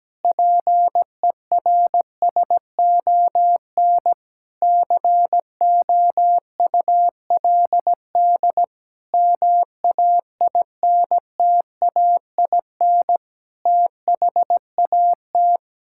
Morse code, heard as a keyed tone, about 17 wpm, 700 Hz. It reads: PERSON COULD MAINTAIN THAT
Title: Sentences from Top 1000 Words - Review Part 5 - 17wpm